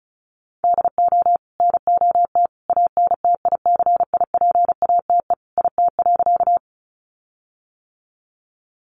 Morse code, heard as a keyed tone, about 34 wpm, 700 Hz. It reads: DO NOT ANTICIPATE IT.
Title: Sentences from Top 1500 Words 35wpm